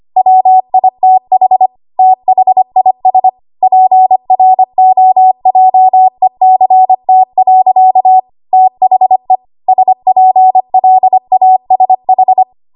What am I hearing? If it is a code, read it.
Morse code, heard as a keyed tone, about 25 wpm, 750 Hz. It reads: WITH THIS PROJECT. THE SPLASH
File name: Archive / 25 WPM / 25WPM